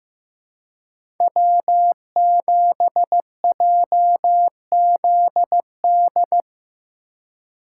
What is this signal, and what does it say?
Morse code, heard as a keyed tone, about 15 wpm, 700 Hz. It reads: W7JZD
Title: USA Call Signs - 5 Characters - 3x Character Spacing - 15wpm